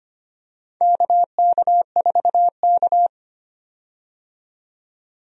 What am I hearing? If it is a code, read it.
Morse code, heard as a keyed tone, about 25 wpm, 700 Hz. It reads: KK4K
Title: USA Call Signs - 4 Characters - 25wpm